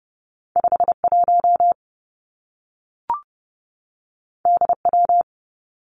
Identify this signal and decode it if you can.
Morse code, heard as a keyed tone, about 30 wpm, 700 Hz. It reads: 5 1 D W
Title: Two Challenging Letter-Number Combinations 30wpm